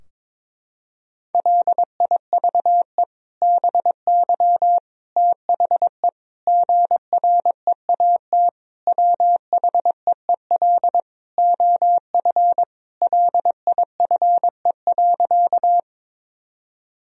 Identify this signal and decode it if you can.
Morse code, heard as a keyed tone, about 22 wpm, 700 Hz. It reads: LIVE BY THE GREAT WHEEL OF LIFE.
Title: Sentences from Top 1000 Words - Review Part 3 - 22wpm